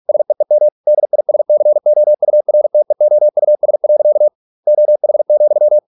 Morse code, heard as a keyed tone, about 46 wpm, 600 Hz. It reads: SEEM DISCOURTEOUS. OH,